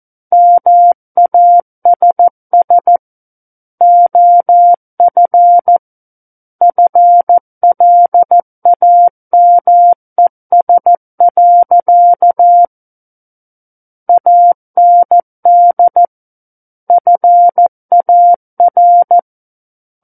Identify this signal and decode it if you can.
Morse code, heard as a keyed tone, about 14 wpm, 700 Hz. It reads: MASS OF FLAMES. AND FAR
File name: Book One / 05-Chapter 5 - 14 WPM